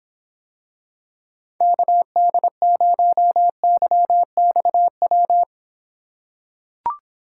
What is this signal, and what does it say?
Morse code, heard as a keyed tone, about 26 wpm, 700 Hz. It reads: KD0YXW